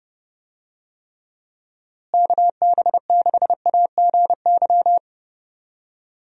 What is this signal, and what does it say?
Morse code, heard as a keyed tone, about 30 wpm, 700 Hz. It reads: KB6AGY
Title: USA Call Signs - 6 Characters - Repeated 3x - 30wpm